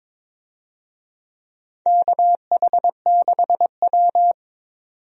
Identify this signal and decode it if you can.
Morse code, heard as a keyed tone, about 22 wpm, 700 Hz. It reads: KH6W
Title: DXCC-100 Call Signs - 4 Characters - 22wpm